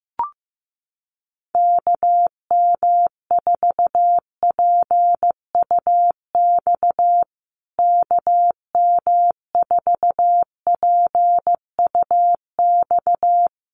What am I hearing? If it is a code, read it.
Morse code, heard as a keyed tone, about 15 wpm, 700 Hz. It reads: KM4PUX KM4PUX
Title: USA Call Signs Repeated 3x 15wpm